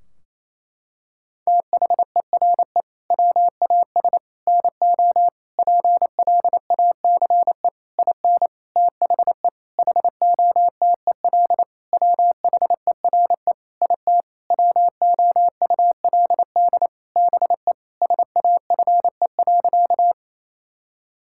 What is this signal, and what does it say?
Morse code, heard as a keyed tone, about 28 wpm, 700 Hz. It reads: THERE WAS NO PLACE IN THE HOTEL WHERE IT WOULD BE SAFE.
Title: Sentences from Top 1600 Words 28wpm